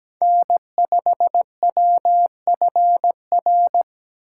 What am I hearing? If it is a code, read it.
Morse code, heard as a keyed tone, about 20 wpm, 700 Hz. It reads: N5WFR
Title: Speed Racing - USA Call Signs 17wpm